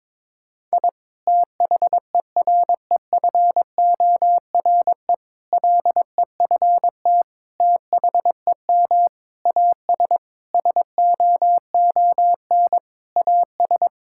Morse code, heard as a keyed tone, about 21 wpm, 700 Hz. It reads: I THEREFORE LEFT THEM AS SOON AS
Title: Sentences from Top 1000 Words - Review Part 5 - 22wpm